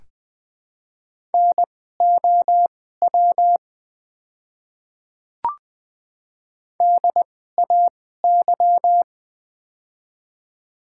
Morse code, heard as a keyed tone, about 20 wpm, spaced as for 15 wpm, 700 Hz. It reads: NOW DAY
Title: 2x Character Spacing - Top 200 Words - 20wpm